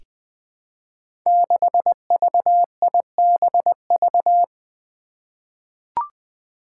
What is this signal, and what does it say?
Morse code, heard as a keyed tone, about 20 wpm, 700 Hz. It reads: 6VIBV
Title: Five Challenging Letter-Number Combinations 20wpm